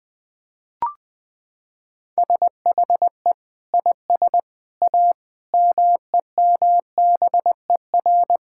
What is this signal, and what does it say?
Morse code, heard as a keyed tone, about 21 wpm, 700 Hz. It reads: SHE IS A MEMBER
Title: Sentences with 5-Word Phrases - 20wpm